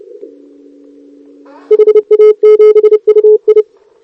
Morse code, heard as a keyed tone, about 27 wpm, 440 Hz. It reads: HA7UI
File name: CQWWCW2023 / ZF5T